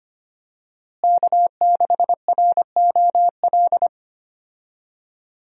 Morse code, heard as a keyed tone, about 25 wpm, 700 Hz. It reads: K6ROL
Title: USA Call Signs - 5 Characters - Repeated 2x - 25wpm